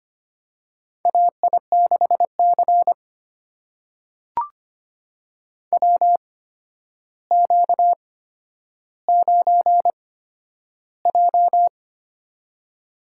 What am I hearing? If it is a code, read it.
Morse code, heard as a keyed tone, about 25 wpm, spaced as for 11 wpm, 700 Hz. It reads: AI6C WQ9J
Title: USA Call Signs - 4 Characters - 8x Character Spacing - 25wpm